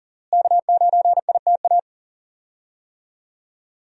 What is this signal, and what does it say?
Morse code, heard as a keyed tone, about 40 wpm, 700 Hz. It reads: K9ITA
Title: USA Call Signs - 5 Characters - Repeated 2x - 40wpm